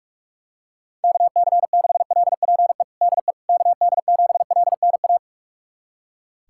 Morse code, heard as a keyed tone, about 42 wpm, 700 Hz. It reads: KC6LPE DE KD7LNA
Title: USA Callsign DE Callsign - 6 Characters - 45wpm